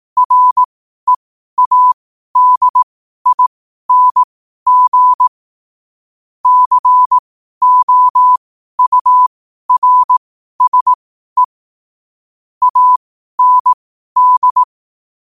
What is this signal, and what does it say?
Morse code, heard as a keyed tone, about 18 wpm, spaced as for 12 wpm, 1000 Hz. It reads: READING COURSE AND
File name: qotdfiles / 12wpm